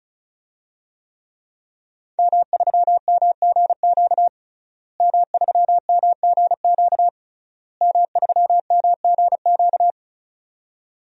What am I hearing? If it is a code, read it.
Morse code, heard as a keyed tone, about 36 wpm, 700 Hz. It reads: M3MGQ M3MGQ M3MGQ
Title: DXCC-100 Call Signs - 5 Characters - Repeated 3x - 35wpm